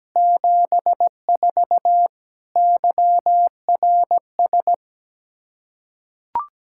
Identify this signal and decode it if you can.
Morse code, heard as a keyed tone, about 17 wpm, 700 Hz. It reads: 74 YRS
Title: K1USN SST QSO Practice 17wpm